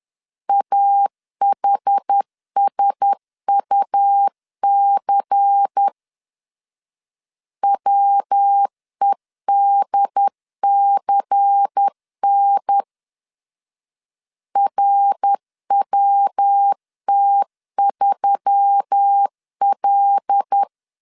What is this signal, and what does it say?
Morse code, heard as a keyed tone, about 10 wpm, 800 Hz. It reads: AHSUC WEDCN RWT3L
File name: LEZIONI / 6 PKX0 / 6 P 10wpm 2432